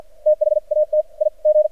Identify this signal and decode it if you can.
Morse code, heard as a keyed tone, about 48 wpm, 610 Hz, with radio noise. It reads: THAT I G
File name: archived / CRSnet